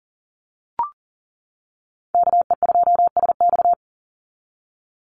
Morse code, heard as a keyed tone, about 40 wpm, 700 Hz. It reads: KE2SX
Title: Call Signs - 40wpm